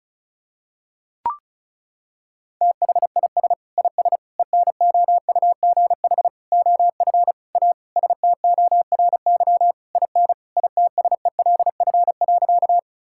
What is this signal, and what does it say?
Morse code, heard as a keyed tone, about 36 wpm, 700 Hz. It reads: THIS IS ENOUGH OF A STORY IN ITSELF.
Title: Sentences from Top 1000 Words - Review Part 5 - 35wpm